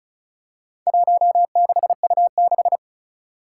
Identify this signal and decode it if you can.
Morse code, heard as a keyed tone, about 36 wpm, 700 Hz. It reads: 16U6
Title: Four Challenging Letter-Number Combinations 35wpm